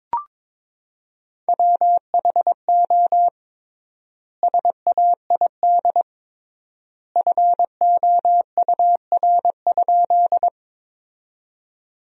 Morse code, heard as a keyed tone, about 22 wpm, 700 Hz. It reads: WHO SAID FOUR?
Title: 3x Word-Spacing - Sentences from Top 300 Words - 22wpm